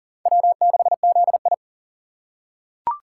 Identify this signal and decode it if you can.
Morse code, heard as a keyed tone, about 40 wpm, 700 Hz. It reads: W6ZI
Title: USA Call Signs - 4 Characters - 4x Character Spacing - 40wpm